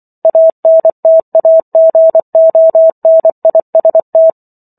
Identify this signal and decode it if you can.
Morse code, heard as a keyed tone, about 24 wpm, 650 Hz. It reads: ANTAGONIST